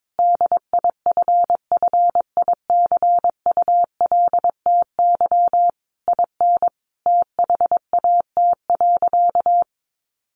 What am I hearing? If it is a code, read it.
Morse code, heard as a keyed tone, about 22 wpm, 700 Hz. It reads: DIFFICULTY IN THAT.
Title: Sentences from Top 1500 Words 22wpm